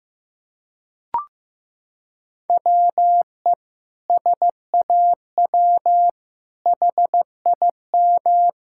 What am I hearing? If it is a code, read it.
Morse code, heard as a keyed tone, about 15 wpm, 700 Hz. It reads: WE SAW HIM
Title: Sentences from Top 1000 Words - Review Part 6 - 15wpm